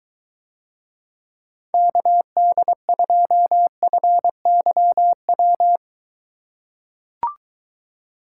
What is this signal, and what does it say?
Morse code, heard as a keyed tone, about 23 wpm, 700 Hz. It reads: KD2FYW